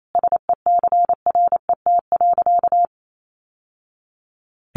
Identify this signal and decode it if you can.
Morse code, heard as a keyed tone, about 28 wpm, 700 Hz. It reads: SECRET.
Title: Sentences from Top 1200 Words 28wpm